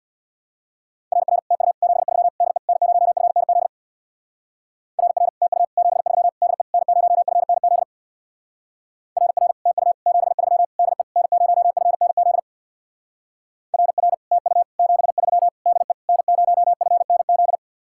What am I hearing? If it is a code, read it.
Morse code, heard as a keyed tone, about 64 wpm, 700 Hz. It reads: RR TU 73 DE N0FNZ RR TU 73 DE N0FNZ RR TU 73 DE N0FNZ RR TU 73 DE N0FNZ